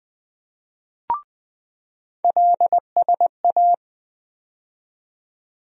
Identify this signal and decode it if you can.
Morse code, heard as a keyed tone, about 20 wpm, 700 Hz. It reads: LSA
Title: USA Call Sign Postfixes - 3 Letters 20wpm